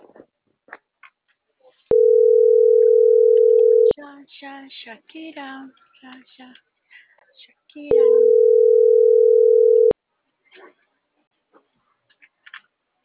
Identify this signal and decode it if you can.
Ringback tone (North America), 440 + 480 Hz, with speech.